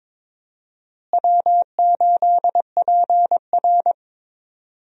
Morse code, heard as a keyed tone, about 22 wpm, 700 Hz. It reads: W8PR
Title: USA Call Signs - 4 Characters - Repeated 2x - 22wpm